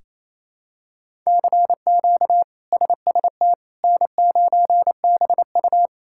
Morse code, heard as a keyed tone, about 28 wpm, 700 Hz. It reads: CQ SST N9BU
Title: K1USN SST QSO Practice - 28wpm